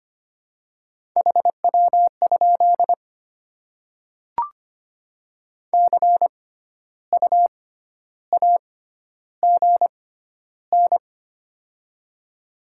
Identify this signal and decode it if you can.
Morse code, heard as a keyed tone, about 25 wpm, spaced as for 12 wpm, 700 Hz. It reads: HW? CU AGN
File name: Top 20 QSO Elements - 6x Character Spacing - 25wpm